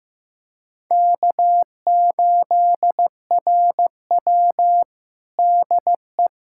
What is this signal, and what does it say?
Morse code, heard as a keyed tone, about 15 wpm, 700 Hz. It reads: K8RW DE
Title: USA Callsign DE Callsign - 4 Characters - 15wpm